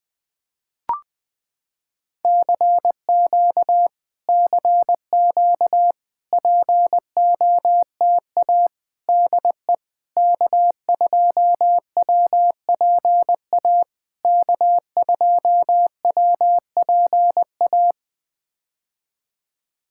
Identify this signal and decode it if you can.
Morse code, heard as a keyed tone, about 20 wpm, 700 Hz. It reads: CQ CQ POTA DE K2WPA K2WPA
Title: POTA QSO Practice - 20wpm